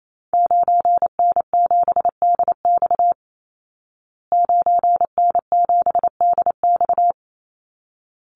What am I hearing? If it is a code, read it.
Morse code, heard as a keyed tone, about 28 wpm, 700 Hz. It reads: 9N7DX 9N7DX
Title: Call Signs Repeated 3x - 28wpm